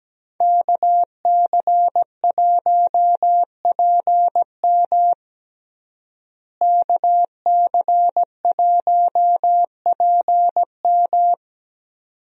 Morse code, heard as a keyed tone, about 17 wpm, 700 Hz. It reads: KC1PM KC1PM
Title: USA Call Signs - 5 Characters - Repeated 2x - 17wpm